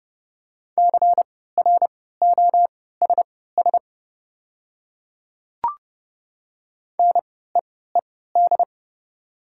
Morse code, heard as a keyed tone, about 30 wpm, spaced as for 17 wpm, 700 Hz. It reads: CROSS NEED
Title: 3x Character Spacing - Top 300 Words - 30wpm